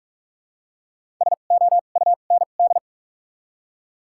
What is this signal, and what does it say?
Morse code, heard as a keyed tone, about 42 wpm, spaced as for 31 wpm, 700 Hz. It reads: SOUND
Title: 2x Character Spacing - Top 100 Words - 45wpm